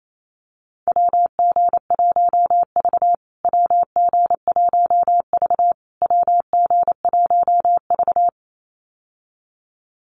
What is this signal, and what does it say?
Morse code, heard as a keyed tone, about 28 wpm, 700 Hz. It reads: WG1V WG1V WG1V
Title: USA Call Signs Repeated 3x 28wpm